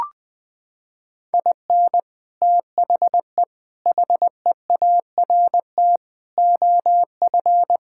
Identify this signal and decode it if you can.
Morse code, heard as a keyed tone, about 20 wpm, 700 Hz. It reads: IN THE HEART OF
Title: Sentences with 3-Word Phrases - 20wpm